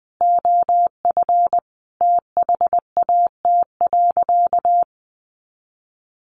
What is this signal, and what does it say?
Morse code, heard as a keyed tone, about 20 wpm, 700 Hz. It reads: OF THAT.
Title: Sentences from Top 300 Words 20wpm